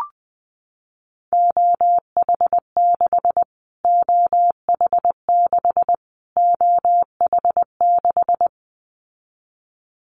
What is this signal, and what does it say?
Morse code, heard as a keyed tone, about 19 wpm, 700 Hz. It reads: OH6 OH6 OH6
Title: Callsign Prefixes for Top 25 DXCC - Repeated 3x - 20wpm